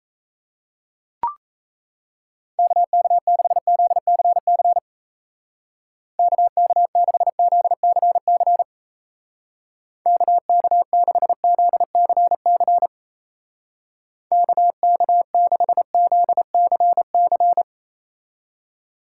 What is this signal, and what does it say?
Morse code, heard as a keyed tone, about 37 wpm, 700 Hz. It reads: KK6ZCC KK6ZCC KK6ZCC KK6ZCC